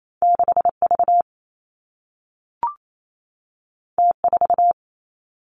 Morse code, heard as a keyed tone, about 28 wpm, 700 Hz. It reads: 6 V T 4
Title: Two Challenging Letter-Number Combinations 28wpm